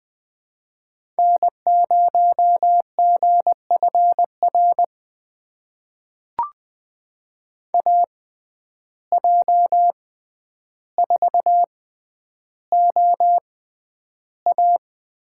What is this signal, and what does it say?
Morse code, heard as a keyed tone, about 21 wpm, spaced as for 10 wpm, 700 Hz. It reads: N0GFR AJ4OA
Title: USA Call Signs - 5 Characters - 6x Character Spacing - 20wpm